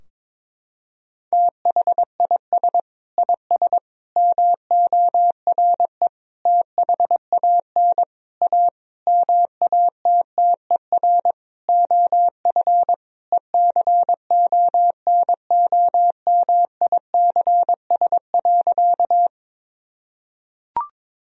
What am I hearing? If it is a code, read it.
Morse code, heard as a keyed tone, about 22 wpm, 700 Hz. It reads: THIS IS MORE THAN A MATTER OF ECONOMICS.
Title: Sentences from Top 1000 Words - Review Part 3 - 22wpm